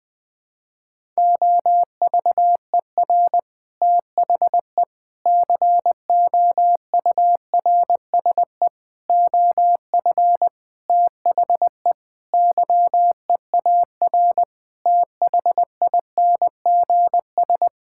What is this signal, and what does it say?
Morse code, heard as a keyed tone, about 20 wpm, 700 Hz. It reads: OVER THE COURSE OF THE YEAR THINGS
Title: Sentences with 5-Word Phrases - 20wpm